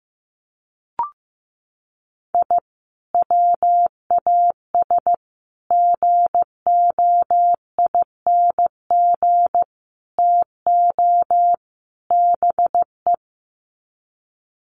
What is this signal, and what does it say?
Morse code, heard as a keyed tone, about 15 wpm, 700 Hz. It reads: I WAS GOING TO BE
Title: Sets of 5 Words 15wpm